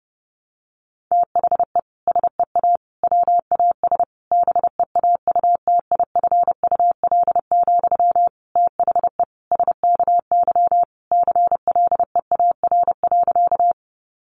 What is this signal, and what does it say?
Morse code, heard as a keyed tone, about 30 wpm, 700 Hz. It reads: THE SEA WAS BEAUTIFUL, THE SKY CLEAR.
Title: Sentences from Top 1200 Words 30wpm